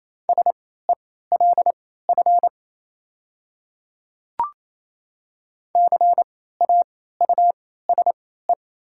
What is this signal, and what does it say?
Morse code, heard as a keyed tone, about 28 wpm, spaced as for 16 wpm, 700 Hz. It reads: SELF CAUSE
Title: 3x Character Spacing - Top 300 Words - 28wpm